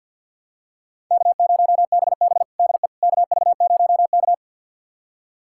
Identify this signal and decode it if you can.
Morse code, heard as a keyed tone, about 51 wpm, 700 Hz. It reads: K0BB DE KV0K
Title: USA Callsign DE Callsign - 4 Characters - 50wpm